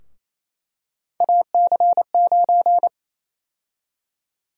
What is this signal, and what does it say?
Morse code, heard as a keyed tone, about 28 wpm, 700 Hz. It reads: AC9
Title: USA Call Sign Prefixes - 28wpm